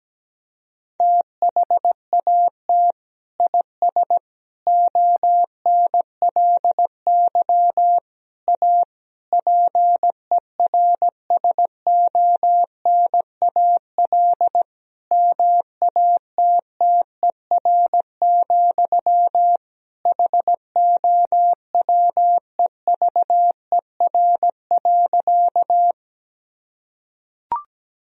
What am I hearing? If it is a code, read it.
Morse code, heard as a keyed tone, about 17 wpm, 700 Hz. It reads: THAT IS ONLY A PERSONAL MATTER, HOWEVER.
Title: Sentences from Top 1000 Words - Review Part 4 - 17wpm